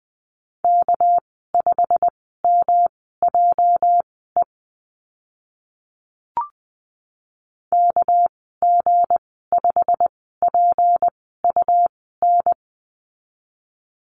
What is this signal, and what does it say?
Morse code, heard as a keyed tone, about 20 wpm, spaced as for 15 wpm, 700 Hz. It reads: K5MJE KG5PUN
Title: 219 Lesson 35g - Number 0 - Call Signs 20wpm